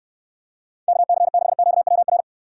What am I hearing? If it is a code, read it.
Morse code, heard as a keyed tone, about 64 wpm, 700 Hz. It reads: KC6QWR